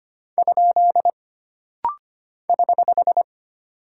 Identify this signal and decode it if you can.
Morse code, heard as a keyed tone, about 25 wpm, 700 Hz. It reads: ? <HH>
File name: Single Punctuation-Prosign - Rapid-Fire - 25wpm